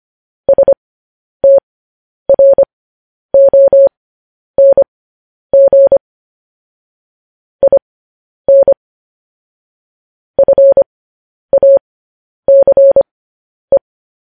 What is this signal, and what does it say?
Morse code, heard as a keyed tone, about 25 wpm, spaced as for 10 wpm, 550 Hz. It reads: STRONG IN FACE
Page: news-headlines-in-morse-code-at-10-wpm